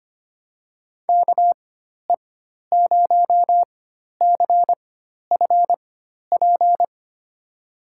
Morse code, heard as a keyed tone, about 25 wpm, spaced as for 12 wpm, 700 Hz. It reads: KE0CFP
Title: USA Call Signs - 6 Characters - 4x Character Spacing - 25wpm